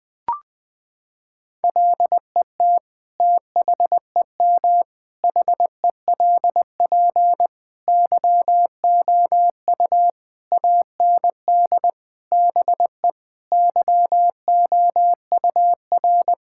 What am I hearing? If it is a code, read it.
Morse code, heard as a keyed tone, about 20 wpm, 700 Hz. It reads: LET THEM HELP YOU AND BE YOUR
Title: Sentences from Top 1600 Words 20wpm